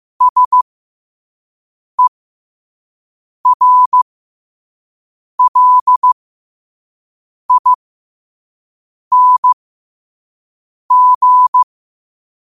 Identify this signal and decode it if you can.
Morse code, heard as a keyed tone, about 15 wpm, spaced as for 5 wpm, 1000 Hz. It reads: SERLING
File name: qotdfiles / 5wpm